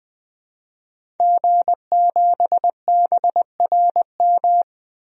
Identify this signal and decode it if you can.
Morse code, heard as a keyed tone, about 20 wpm, 700 Hz. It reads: G7BRM
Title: DXCC-100 Call Signs - 5 Characters - 20wpm